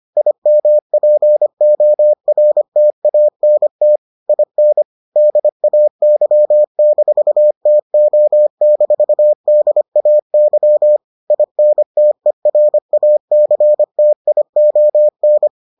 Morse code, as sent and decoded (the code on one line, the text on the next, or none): .. -- .--. --- .-. - .- -. - / .. -. / -.. .- -.-- -....- - --- -....- -.. .- -.-- / .. -. - . .-. .- -.-. - .. --- -.
IMPORTANT IN DAY-TO-DAY INTERACTION